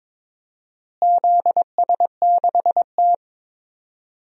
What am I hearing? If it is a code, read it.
Morse code, heard as a keyed tone, about 23 wpm, 700 Hz. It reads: ZS6T
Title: DXCC-100 Call Signs - 4 Characters - Repeated 3x - 22wpm